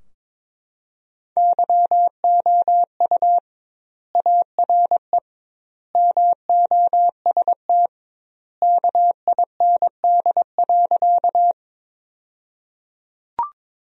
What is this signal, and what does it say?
Morse code, heard as a keyed tone, about 22 wpm, 700 Hz. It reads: YOU ARE MOST KIND.
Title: 2x Word-Spacing - Sentences from Top 200 Words - 22wpm